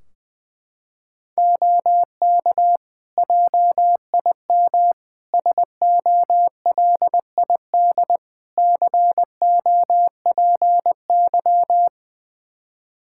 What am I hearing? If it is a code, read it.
Morse code, heard as a keyed tone, about 20 wpm, 700 Hz. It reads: OK JIM SOLID COPY
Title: Practice QSO - Exchange 3 - More Info - 20wpm